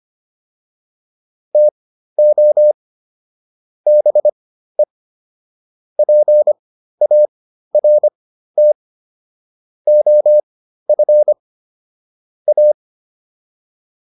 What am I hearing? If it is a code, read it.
Morse code, heard as a keyed tone, about 25 wpm, spaced as for 13 wpm, 600 Hz. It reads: TO BE PART OF A